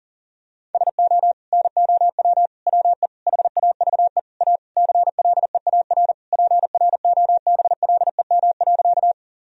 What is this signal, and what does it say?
Morse code, heard as a keyed tone, about 40 wpm, 700 Hz. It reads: SO NOW WE HAVE A CLEAR PROBLEM.
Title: Sentences from Top 1000 Words - Review Part 2 - 40wpm